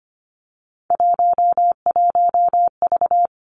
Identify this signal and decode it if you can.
Morse code, heard as a keyed tone, about 25 wpm, 700 Hz. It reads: 11V
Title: Three Challenging Letter-Number Combinations 25wpm